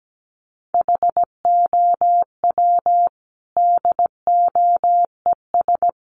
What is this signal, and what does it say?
Morse code, heard as a keyed tone, about 17 wpm, 700 Hz. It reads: HOW DOES
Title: Sentences from Top 200 Words 17wpm